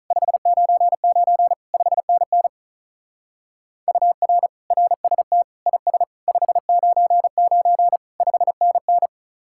Morse code, heard as a keyed tone, about 44 wpm, 700 Hz. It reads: 599 5NN UR RST IS 599 5NN